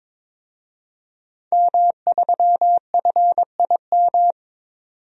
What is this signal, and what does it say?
Morse code, heard as a keyed tone, about 22 wpm, 700 Hz. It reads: M3FIM
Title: DXCC-100 Call Signs - 5 Characters - 22wpm